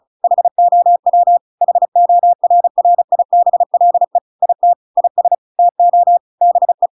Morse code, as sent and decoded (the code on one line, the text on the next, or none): .... --- .-- / .... --- .-. .-. .. -... .-.. / .. - / .. ... / - --- / -... .
HOW HORRIBLE IT IS TO BE